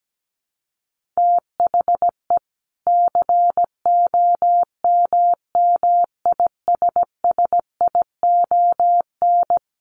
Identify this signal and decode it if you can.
Morse code, heard as a keyed tone, about 17 wpm, 700 Hz. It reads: THE COMMISSION
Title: Sentences from Top 1400 Words 17wpm